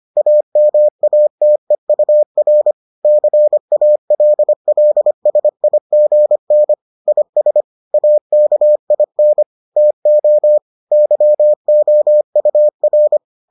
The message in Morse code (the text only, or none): AMATEUR CALLSIGN IS AKIN TO YOUR